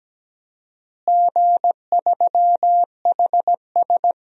Morse code, heard as a keyed tone, about 17 wpm, 700 Hz. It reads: G3HS
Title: DXCC-100 Call Signs - 4 Characters - Repeated 3x - 17wpm